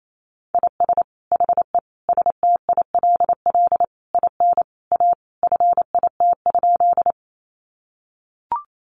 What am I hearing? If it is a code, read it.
Morse code, heard as a keyed tone, about 28 wpm, 700 Hz. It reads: IS HE STILL IN A FIT?
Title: Sentences from Top 800 Words 28wpm